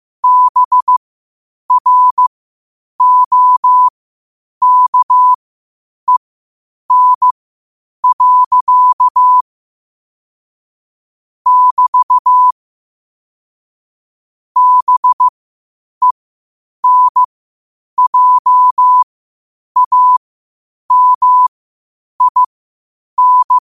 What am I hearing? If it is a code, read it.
Morse code, heard as a keyed tone, about 15 wpm, spaced as for 8 wpm, 1000 Hz. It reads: BROKEN. <BT> BENJAMIN
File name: qotdfiles / 8wpm